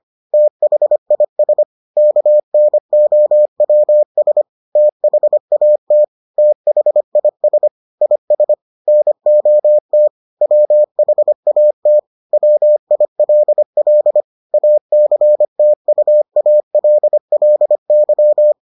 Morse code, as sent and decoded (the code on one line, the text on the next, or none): - .... .. ... / -.- -. --- .-- ... / - .... .- - / - .... .. ... / .. ... / -. --- - / .-- .... .- - / .-- .. .-.. .-.. / .- -.-. - ..- .- .-.. .-.. -.--
THIS KNOWS THAT THIS IS NOT WHAT WILL ACTUALLY